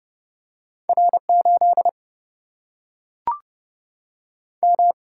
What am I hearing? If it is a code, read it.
Morse code, heard as a keyed tone, about 30 wpm, 700 Hz. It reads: R 8 M
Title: Single-Letter Single-Number 30wpm